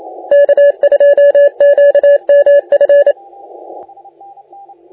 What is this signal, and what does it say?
Morse code, heard as a keyed tone, about 31 wpm, 570 Hz. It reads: K2QMF
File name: la4c / sac-cw-2018 / CONTEST AUDIO 1600Z